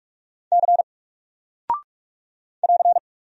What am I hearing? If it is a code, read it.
Morse code, heard as a keyed tone, about 44 wpm, 700 Hz. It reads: C <AR>
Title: Single Character (Letter:Number:Punctuation:Prosign) Alt 45wpm